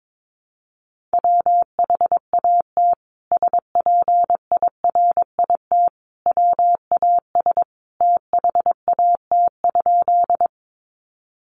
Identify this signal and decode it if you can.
Morse code, heard as a keyed tone, about 22 wpm, 700 Hz. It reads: WHAT SPIRIT WAS THAT?
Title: Sentences from Top 1400 Words 22wpm